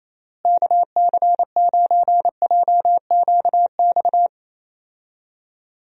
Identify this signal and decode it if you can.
Morse code, heard as a keyed tone, about 28 wpm, 700 Hz. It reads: KC9JQX
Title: USA Call Signs - 6 Characters - Repeated 3x - 28wpm